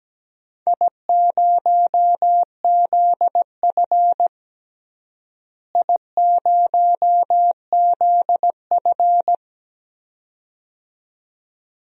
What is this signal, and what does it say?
Morse code, heard as a keyed tone, about 17 wpm, 700 Hz. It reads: I0ZF I0ZF
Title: DXCC-100 Call Signs - 4 Characters - Repeated 2x - 17wpm